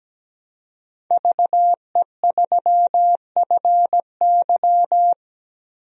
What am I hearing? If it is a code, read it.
Morse code, heard as a keyed tone, about 17 wpm, 700 Hz. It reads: VE3FY
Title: DXCC-100 Call Signs - 5 Characters - 17wpm